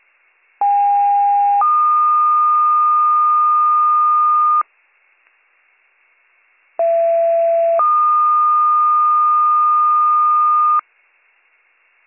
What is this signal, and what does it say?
Two-tone page, heard as a one-second tone, then a three-second tone, about 810 then 1220 Hz, about 670 then 1220 Hz. It is heard from 0.6 to 10.8 s.